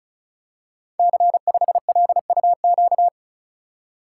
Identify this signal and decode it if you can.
Morse code, heard as a keyed tone, about 36 wpm, 690 Hz. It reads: C5LUQ
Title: Code Groups - 5 Letter:Numbers - 35wpm